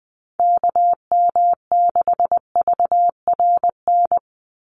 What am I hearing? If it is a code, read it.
Morse code, heard as a keyed tone, about 20 wpm, 700 Hz. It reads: KM6VRN
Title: USA Call Signs 20wpm